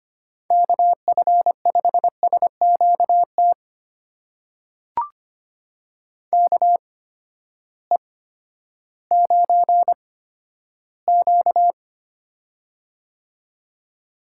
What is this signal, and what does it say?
Morse code, heard as a keyed tone, about 25 wpm, spaced as for 13 wpm, 700 Hz. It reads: KF5SQT KE9Q
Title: USA Call Signs - 8x Character Spacing - 25wpm